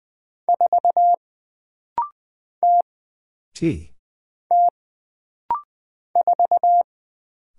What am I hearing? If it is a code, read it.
Morse code, heard as a keyed tone, about 20 wpm, 700 Hz. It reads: 4 T T 4
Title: Single Character (Letter:Number:Punctuation:Prosign) - Rapid-Fire - 20wpm